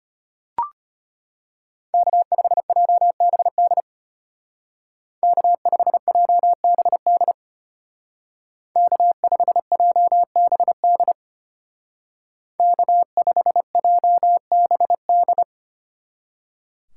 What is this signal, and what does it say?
Morse code, heard as a keyed tone, about 30 wpm, 700 Hz. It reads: K5JBD K5JBD K5JBD K5JBD
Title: Speed Racing - USA Call Signs 25wpm